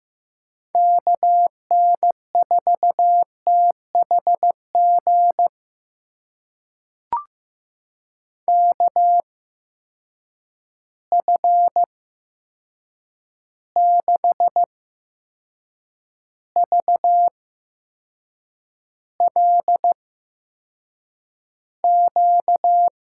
Morse code, heard as a keyed tone, about 15 wpm, spaced as for 6 wpm, 700 Hz. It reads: KN4THG KF6VLQ
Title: USA Call Signs - 6 Characters - 8x Character Spacing - 15wpm